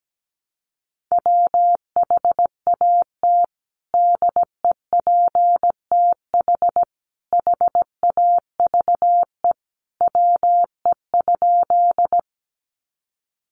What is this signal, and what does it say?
Morse code, heard as a keyed tone, about 17 wpm, 700 Hz. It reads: WHAT DEPTH HAVE WE?
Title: Sentences from Top 1200 Words 17wpm